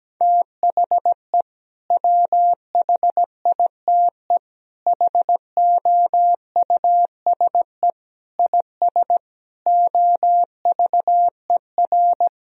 Morse code, heard as a keyed tone, about 17 wpm, 700 Hz. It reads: THE WHITE HOUSE IS OVER
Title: Sentences from Top 1000 Words - Review Part 1 - 17wpm